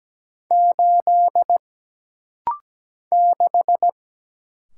Morse code, heard as a keyed tone, about 17 wpm, 700 Hz. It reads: 8 6
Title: Single Letter-Number - Rapid-Fire 17wpm